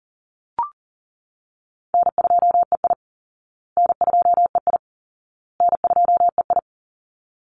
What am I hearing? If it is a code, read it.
Morse code, heard as a keyed tone, about 40 wpm, 700 Hz. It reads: N2EI N2EI N2EI
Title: Call Signs Repeated 2x - 40wpm